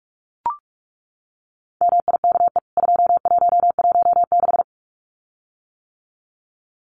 Morse code, heard as a keyed tone, about 44 wpm, 700 Hz. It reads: MIKE 2116
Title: CWT Exchange 45wpm